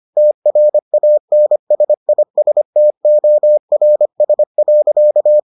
Morse code, heard as a keyed tone, about 25 wpm, 600 Hz. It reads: TRANSISTORS.